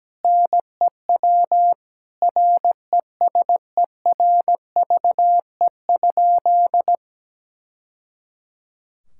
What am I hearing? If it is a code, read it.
Morse code, heard as a keyed tone, about 17 wpm, 700 Hz. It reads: NEW RESERVE?
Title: Sentences from Top 1000 Words - Review Part 6 - 17wpm